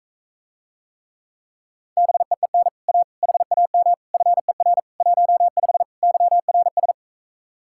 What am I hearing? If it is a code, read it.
Morse code, heard as a keyed tone, about 44 wpm, 700 Hz. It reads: BEEN A HAM FER 15 YRS